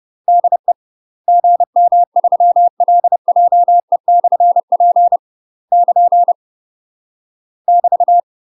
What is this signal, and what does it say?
Morse code, heard as a keyed tone, about 30 wpm, 700 Hz. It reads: DE GM3LJE/P <KN> <BT>
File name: UK MASTER QSOs TRAINING / 30WPM / UK MASTER QSOs 0043